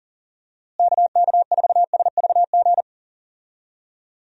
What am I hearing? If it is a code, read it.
Morse code, heard as a keyed tone, about 40 wpm, 700 Hz. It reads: KK4SVG
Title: USA Call Signs - 6 Characters - Repeated 3x - 40wpm